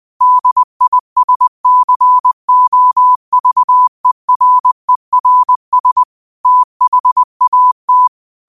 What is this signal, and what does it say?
Morse code, heard as a keyed tone, about 20 wpm, 1000 Hz. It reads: DISCOVERERS THAT
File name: qotdfiles / 20wpm